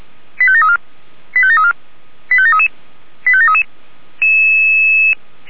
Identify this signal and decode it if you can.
Five-tone selective call (ZVEI), heard as quick runs of five tones, tones of about 70 ms. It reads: 85734 85734 85733 85733
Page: SDR/ZVEI